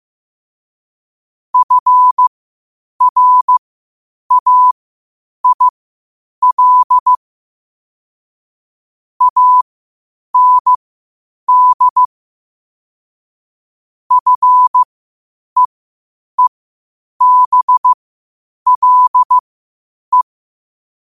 Morse code, heard as a keyed tone, about 15 wpm, spaced as for 8 wpm, 1000 Hz. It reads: FRAIL AND FEEBLE
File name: qotdfiles / 8wpm